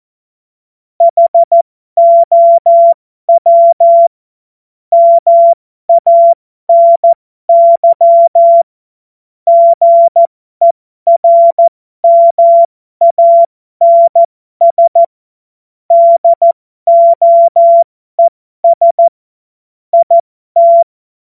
Morse code, heard as a keyed tone, about 14 wpm, spaced as for 12 wpm, 680 Hz. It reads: HOW MANY GERMANS DOES IT
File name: Morsekurs / Witze